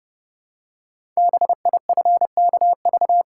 Morse code, heard as a keyed tone, about 30 wpm, 700 Hz. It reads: BIFKV
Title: Code Groups - 5 Letter:Numbers - 30wpm